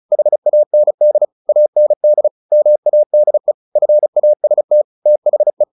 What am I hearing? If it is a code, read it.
Morse code, heard as a keyed tone, about 34 wpm, 600 Hz. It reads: HAND AND MADE FAST THE
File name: the old man and the sea / the old man and the sea 1 35wpm